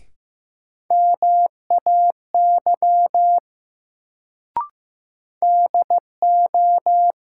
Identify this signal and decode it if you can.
Morse code, heard as a keyed tone, about 15 wpm, 700 Hz. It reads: MAY DO
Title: Sentences from Top 100 Words - New Word Drill - 15wpm